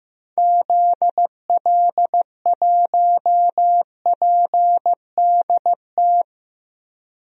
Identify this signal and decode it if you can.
Morse code, heard as a keyed tone, about 16 wpm, 700 Hz. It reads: ZL1PDT
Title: DXCC-100 Call Signs - 6 Characters - 15wpm